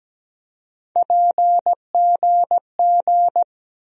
Morse code, heard as a keyed tone, about 17 wpm, 700 Hz. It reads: PGG
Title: USA Call Sign Postfixes - 3 Letters 17wpm